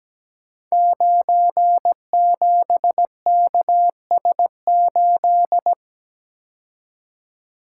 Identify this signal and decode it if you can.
Morse code, heard as a keyed tone, about 17 wpm, 700 Hz. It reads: 97KS8
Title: Code Groups - 5 Letter:Numbers - 17wpm